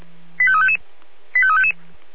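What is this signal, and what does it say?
Five-tone selective call (ZVEI), heard as quick runs of five tones, tones of about 70 ms. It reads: 85355 85355
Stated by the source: ZVEI five-tone selective call received by SDR.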